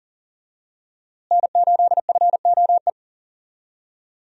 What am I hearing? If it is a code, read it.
Morse code, heard as a keyed tone, about 40 wpm, 700 Hz. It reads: N8FOE
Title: USA Call Signs - 5 Characters - Repeated 2x - 40wpm